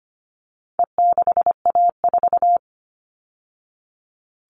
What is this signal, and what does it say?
Morse code, heard as a keyed tone, about 25 wpm, 700 Hz. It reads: E6A4
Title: All Cut Numbers - 4-Digit 25wpm